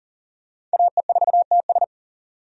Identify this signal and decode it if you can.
Morse code, heard as a keyed tone, about 38 wpm, 700 Hz. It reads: AE4TS
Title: USA Call Signs - 5 Characters - 40wpm